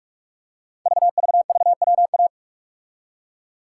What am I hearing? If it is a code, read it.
Morse code, heard as a keyed tone, about 44 wpm, 690 Hz. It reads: VVVWA
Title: Code Groups - 5 Letters - 45wpm